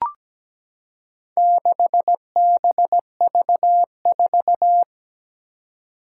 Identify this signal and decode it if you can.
Morse code, heard as a keyed tone, about 17 wpm, 700 Hz. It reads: 6BV4
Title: Four Challenging Letter-Number Combinations 17wpm